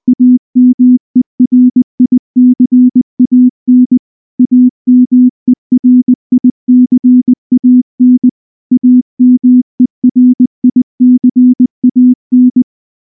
Morse code, heard as a keyed tone, about 20 wpm, 250 Hz. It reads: AMERICAN AMERICAN AMERICAN
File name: morse / morse-training-20WPM-9